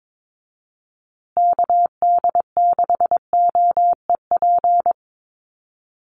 Morse code, heard as a keyed tone, about 22 wpm, 700 Hz. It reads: KD6OEP
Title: USA Call Signs Repeated 2x 22wpm